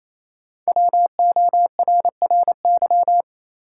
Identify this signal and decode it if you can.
Morse code, heard as a keyed tone, about 28 wpm, 700 Hz. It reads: WORRY
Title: Sets of 5 Words - New Word Drill - 28wpm